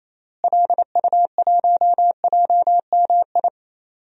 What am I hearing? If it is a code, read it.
Morse code, heard as a keyed tone, about 27 wpm, 700 Hz. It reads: LU1JMI
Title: DXCC-100 Call Signs - 6 Characters - Repeated 2x - 28wpm